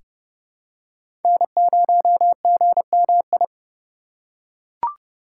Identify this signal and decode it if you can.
Morse code, heard as a keyed tone, about 30 wpm, 700 Hz. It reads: N0GMI
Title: USA Call Signs - 5 Characters - 2x Character Spacing - 30wpm